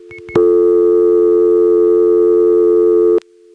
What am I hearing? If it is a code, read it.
Dial tone (North America), 350 + 440 Hz.